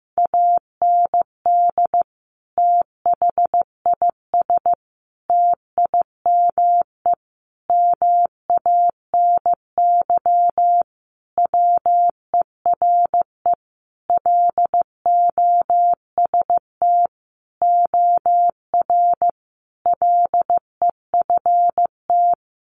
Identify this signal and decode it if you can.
Morse code, heard as a keyed tone, about 15 wpm, 700 Hz. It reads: AND THIS TIME MANY WERE LOST OR LEFT
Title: Sentences from Top 1400 Words 15wpm